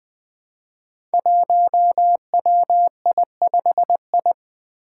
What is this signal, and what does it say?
Morse code, heard as a keyed tone, about 20 wpm, 700 Hz. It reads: 1WI5I
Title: Five Challenging Letter-Number Combinations 20wpm